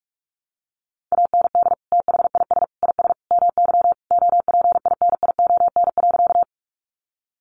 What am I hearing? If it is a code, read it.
Morse code, heard as a keyed tone, about 44 wpm, 700 Hz. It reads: AND THIS IS MY OPINION.
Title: Sentences from Top 1200 Words 45wpm